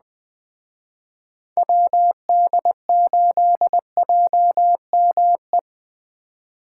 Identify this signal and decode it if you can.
Morse code, heard as a keyed tone, about 21 wpm, 700 Hz. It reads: WD8JME